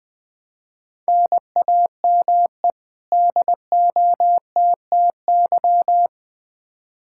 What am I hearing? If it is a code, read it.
Morse code, heard as a keyed tone, about 20 wpm, 700 Hz. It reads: NAME DOTTY
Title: K1USN SST QSO Practice 20wpm